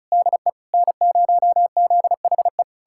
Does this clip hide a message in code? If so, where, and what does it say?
Yes: Morse code from 0.1 to 2.6 s, heard as a keyed tone, about 46 wpm, 700 Hz. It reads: DE N0ZHE